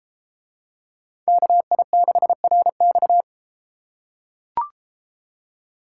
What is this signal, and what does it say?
Morse code, heard as a keyed tone, about 32 wpm, 700 Hz. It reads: KI6RX